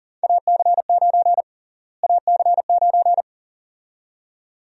Morse code, heard as a keyed tone, about 40 wpm, 700 Hz. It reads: AC9 AC9
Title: USA Call Sign Prefixes - Repeated 2x - 40wpm